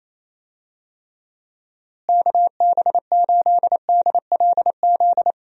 Morse code, heard as a keyed tone, about 28 wpm, 700 Hz. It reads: KB8DLZ
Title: USA Call Signs - 6 Characters - Repeated 3x - 28wpm